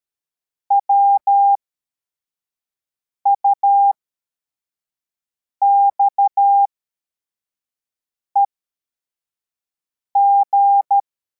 Morse code, heard as a keyed tone, about 13 wpm, spaced as for 4 wpm, 800 Hz. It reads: WUXEG
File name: formation / cw / Text63